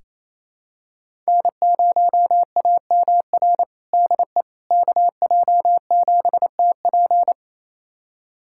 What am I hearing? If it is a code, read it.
Morse code, heard as a keyed tone, about 28 wpm, 700 Hz. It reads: N0AMR DE KJ7TP
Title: USA Callsign DE Callsign - 5 Characters - 28wpm